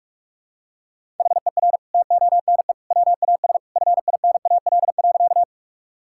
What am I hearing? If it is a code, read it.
Morse code, heard as a keyed tone, about 43 wpm, 700 Hz. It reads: HER TONE WAS FINAL.
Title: Sentences from Top 1600 Words 45wpm